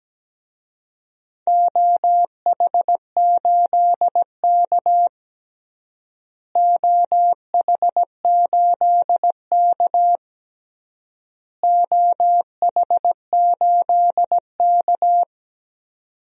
Morse code, heard as a keyed tone, about 17 wpm, 700 Hz. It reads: OH8K OH8K OH8K
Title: DXCC-100 Call Signs - 4 Characters - Repeated 3x - 17wpm